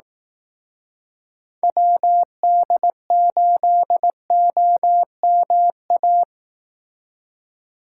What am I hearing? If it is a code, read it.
Morse code, heard as a keyed tone, about 18 wpm, 700 Hz. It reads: WD8OMA
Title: Speed Racing - USA Call Signs 15wpm